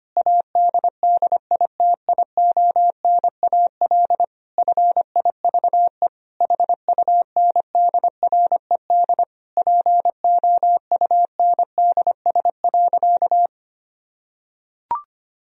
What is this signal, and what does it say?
Morse code, heard as a keyed tone, about 25 wpm, 700 Hz. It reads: ADDITIONAL FIVE HUNDRED POUNDS.
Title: Sentences from Top 1000 Words - Review Part 4 - 25wpm